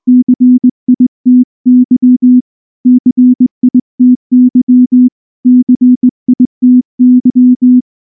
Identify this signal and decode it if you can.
Morse code, heard as a keyed tone, about 19 wpm, 250 Hz. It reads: CITY CITY CITY
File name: morse / morse-training-20WPM